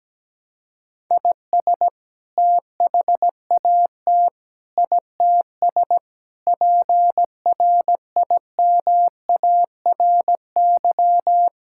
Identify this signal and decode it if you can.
Morse code, heard as a keyed tone, about 17 wpm, 700 Hz. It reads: IS THAT ITS PRIMARY
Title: Sentences from Top 1000 Words - Review Part 5 - 17wpm